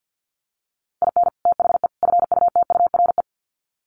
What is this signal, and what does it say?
Morse code, heard as a keyed tone, about 48 wpm, 700 Hz. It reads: IN THE FUTURE
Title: Sets of 3 Words 50wpm